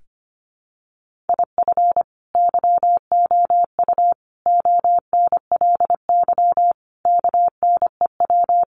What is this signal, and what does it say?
Morse code, heard as a keyed tone, about 25 wpm, 700 Hz. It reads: IF YOU ONLY KNEW
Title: Sentences from Top 1500 Words 25wpm